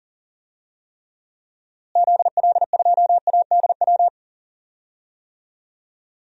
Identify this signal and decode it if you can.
Morse code, heard as a keyed tone, about 40 wpm, 700 Hz. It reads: ZL2ADW
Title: DXCC-100 Call Signs - 6 Characters - Repeated 3x - 40wpm